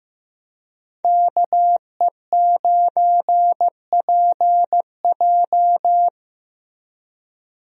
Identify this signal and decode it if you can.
Morse code, heard as a keyed tone, about 15 wpm, 700 Hz. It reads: KE9PJ
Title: USA Call Signs - 5 Characters - Repeated 2x - 15wpm